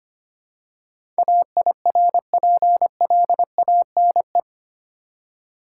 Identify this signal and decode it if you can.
Morse code, heard as a keyed tone, about 25 wpm, 700 Hz. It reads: AIRPLANE
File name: practice / 25wpm / Easy Words with Hints - 25wpm